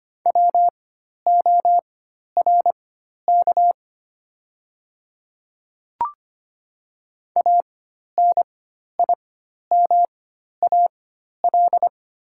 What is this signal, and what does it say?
Morse code, heard as a keyed tone, about 25 wpm, spaced as for 12 wpm, 700 Hz. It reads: WORK ANIMAL
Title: 4x Character Spacing - Top 300 Words - 25wpm